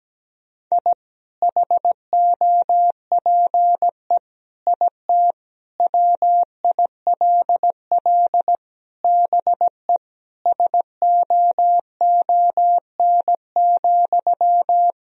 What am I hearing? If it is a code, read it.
Morse code, heard as a keyed tone, about 17 wpm, 700 Hz. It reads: I HOPE IT WILL BE SOON,
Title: Sentences from Top 1000 Words - Review Part 2 - 17wpm